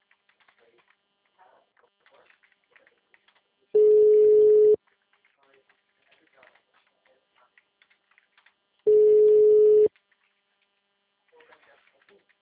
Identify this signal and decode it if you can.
Ringback tone (Europe, 425 Hz), with speech.